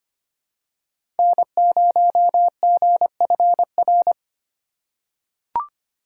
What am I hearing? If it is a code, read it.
Morse code, heard as a keyed tone, about 25 wpm, 700 Hz. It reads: N0GFR
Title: USA Call Signs - 5 Characters - 2x Character Spacing - 25wpm